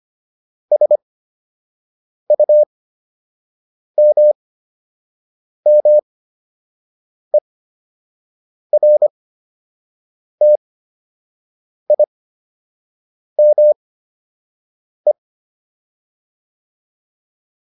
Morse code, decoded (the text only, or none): SUMMERTIME